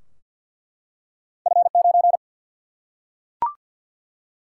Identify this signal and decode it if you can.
Morse code, heard as a keyed tone, about 47 wpm, 700 Hz. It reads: F 9
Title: Single-Letter Single-Number 50wpm